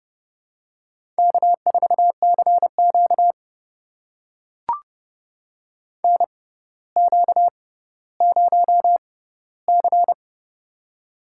Morse code, heard as a keyed tone, about 30 wpm, spaced as for 15 wpm, 700 Hz. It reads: K4CQ NQ0C
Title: USA Call Signs - 4 Characters - 6x Character Spacing - 30wpm